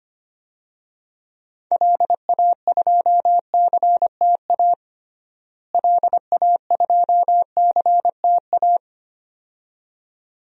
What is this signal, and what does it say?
Morse code, heard as a keyed tone, about 25 wpm, 700 Hz. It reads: LA2CTA LA2CTA
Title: DXCC-100 Call Signs - 6 Characters - Repeated 2x - 25wpm